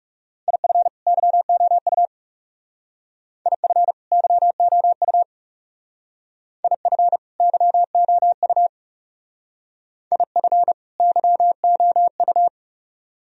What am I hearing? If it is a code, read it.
Morse code, heard as a keyed tone, about 36 wpm, 700 Hz. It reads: IF YOU IF YOU IF YOU IF YOU
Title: Speed-Racing - Sets of 2 Words 30wpm